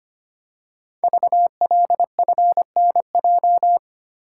Morse code, heard as a keyed tone, about 25 wpm, 700 Hz. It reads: VLFNJ
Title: Code Groups - 5 Letters - 25wpm